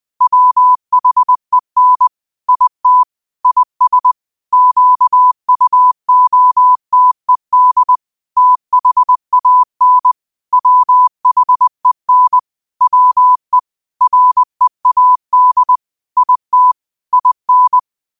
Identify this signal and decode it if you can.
Morse code, heard as a keyed tone, about 20 wpm, 1000 Hz. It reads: WHEN IT IS QUOTED THAN WHEN WE READ IT IN
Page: New quotes every day in morse code at 20 Words per minute.